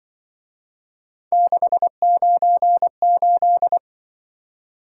Morse code, heard as a keyed tone, about 25 wpm, 700 Hz. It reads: 698
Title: Speed-Racing - Three Letter-Number Combinations - 20wpm